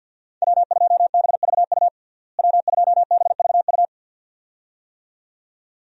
Morse code, heard as a keyed tone, about 52 wpm, 700 Hz. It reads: W2BVU W2BVU
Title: USA Call Signs - 5 Characters - Repeated 2x - 50wpm